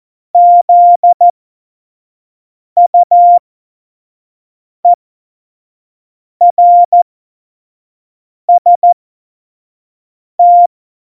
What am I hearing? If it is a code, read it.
Morse code, heard as a keyed tone, about 14 wpm, spaced as for 5 wpm, 700 Hz. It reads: ZUERST
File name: Morsekurs / Witze / Witz 05 Mai 2020, 25 F70